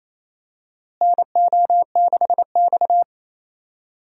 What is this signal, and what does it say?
Morse code, heard as a keyed tone, about 28 wpm, 700 Hz. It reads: NO6X
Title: USA Call Signs - 4 Characters - Repeated 3x - 28wpm